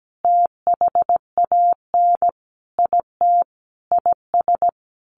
Morse code, heard as a keyed tone, about 17 wpm, 700 Hz. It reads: THAN IT IS
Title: Sentences from Top 1200 Words 17wpm